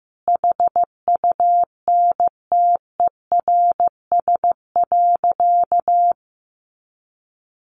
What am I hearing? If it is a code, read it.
Morse code, heard as a keyed tone, about 15 wpm, 700 Hz. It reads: HUNTERS.
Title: A Sound Of Thunder 15wpm